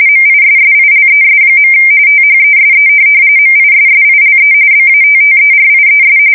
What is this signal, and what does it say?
RTTY, about 45.45 baud. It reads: MISSILES. THESE MISSILE BATTERIES
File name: fsk170 / USS-New-Jersey